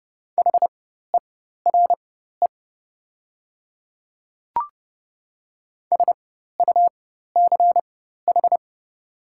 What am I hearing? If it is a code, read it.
Morse code, heard as a keyed tone, about 30 wpm, spaced as for 14 wpm, 700 Hz. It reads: HERE SUCH
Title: 4x Character Spacing - Top 500 Words - 30wpm